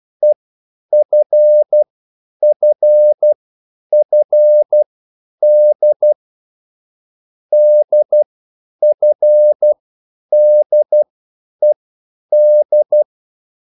Morse code, heard as a keyed tone, about 12 wpm, spaced as for 9 wpm, 600 Hz. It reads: EFFFD DFDED
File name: cwbook